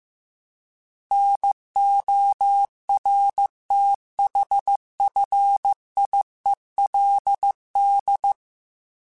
Morse code, heard as a keyed tone, about 15 wpm, 800 Hz. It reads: NORTHFIELD